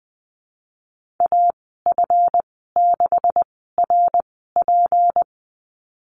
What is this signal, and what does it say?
Morse code, heard as a keyed tone, about 20 wpm, spaced as for 15 wpm, 700 Hz. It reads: AF6RP
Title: 167 Lesson 27g - Number 7 - Call Signs 20wpm